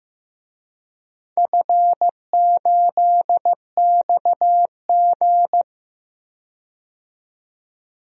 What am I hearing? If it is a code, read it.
Morse code, heard as a keyed tone, about 15 wpm, 700 Hz. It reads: F8XG
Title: DXCC-100 Call Signs - 4 Characters - 15wpm